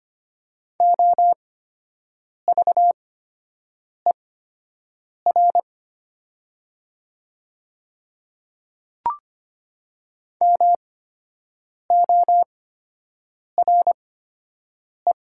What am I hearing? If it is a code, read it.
Morse code, heard as a keyed tone, about 25 wpm, spaced as for 7 wpm, 700 Hz. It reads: OVER MORE
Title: 8x Character Spacing - Top 100 Words - 25wpm